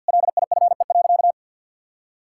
Morse code, heard as a keyed tone, about 50 wpm, 700 Hz. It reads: LIFE.
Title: Sentences from Top 1600 Words 50wpm